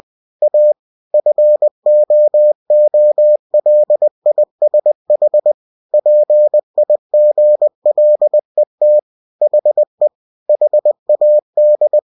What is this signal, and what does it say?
Morse code, heard as a keyed tone, about 20 wpm, 600 Hz. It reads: A FOOLISH PIGLET HE HAD
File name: cw-books / winnie-the-pooh / winnie-20wpm-page-0043